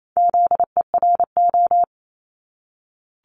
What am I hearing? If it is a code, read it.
Morse code, heard as a keyed tone, about 28 wpm, 700 Hz. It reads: ZERO
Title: Multi-letter Phonemes 28wpm